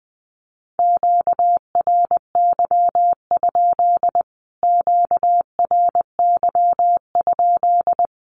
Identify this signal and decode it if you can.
Morse code, heard as a keyed tone, about 20 wpm, 700 Hz. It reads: QRY? QRY?
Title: Q-Codes Repeated 3x 20wpm